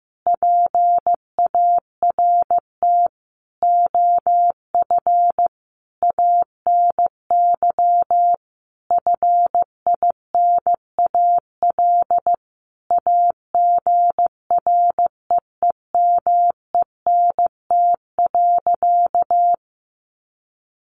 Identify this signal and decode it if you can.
Morse code, heard as a keyed tone, about 15 wpm, 700 Hz. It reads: PART OF ANY FINAL AGREEMENT.
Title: Sentences from Top 1500 Words 15wpm